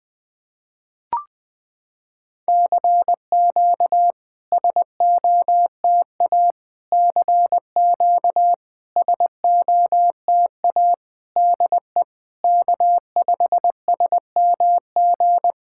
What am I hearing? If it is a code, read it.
Morse code, heard as a keyed tone, about 20 wpm, 700 Hz. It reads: CQ SOTA CQ SOTA DE K5SMG
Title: SOTA QSO Practice 20wpm